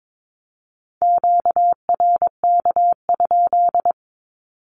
Morse code, heard as a keyed tone, about 22 wpm, 700 Hz. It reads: QRK?
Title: Q-Codes 22wpm